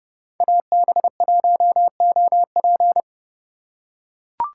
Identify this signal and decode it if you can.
Morse code, heard as a keyed tone, about 29 wpm, 700 Hz. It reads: AB1OP